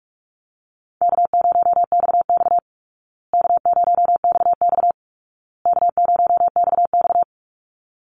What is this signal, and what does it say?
Morse code, heard as a keyed tone, about 48 wpm, 700 Hz. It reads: K0XX K0XX K0XX
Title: Easy Call Signs Repeated 3x - 45wpm